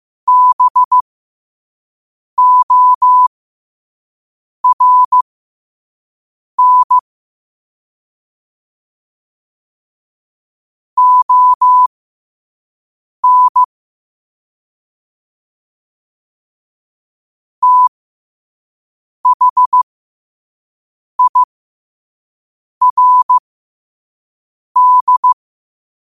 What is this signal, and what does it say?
Morse code, heard as a keyed tone, about 15 wpm, spaced as for 5 wpm, 1000 Hz. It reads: BORN ON THIRD
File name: qotdfiles / 5wpm